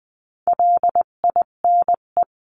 Morse code, heard as a keyed tone, about 20 wpm, 700 Hz. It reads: LINE
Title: 258 Lesson 42c - Words 16-20wpm